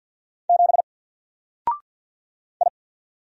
Morse code, heard as a keyed tone, about 49 wpm, 700 Hz. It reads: 7 I
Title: Single Character (Letter:Number:Punctuation:Prosign) Alt 50wpm